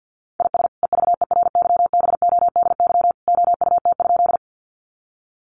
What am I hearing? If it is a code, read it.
Morse code, heard as a keyed tone, about 48 wpm, 700 Hz. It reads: IS EVERYBODY OUT?
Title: Sentences from Top 1400 Words 50wpm